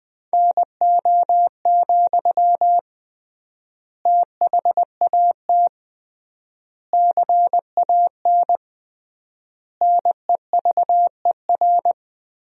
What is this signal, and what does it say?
Morse code, heard as a keyed tone, about 21 wpm, 700 Hz. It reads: NO, THAT CAN NEVER
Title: 3x Word-Spacing - Sentences from Top 300 Words - 20wpm